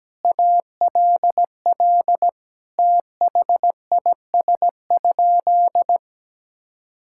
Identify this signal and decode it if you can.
Morse code, heard as a keyed tone, about 17 wpm, 700 Hz. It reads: ALL THIS?
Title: Sentences from Top 1000 Words - Review Part 6 - 17wpm